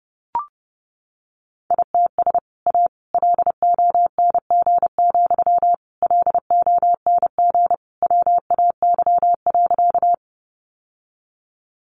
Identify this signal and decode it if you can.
Morse code, heard as a keyed tone, about 28 wpm, 700 Hz. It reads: ITS A LONG, LONG WAY.
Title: Sentences from Top 600 Words 30wpm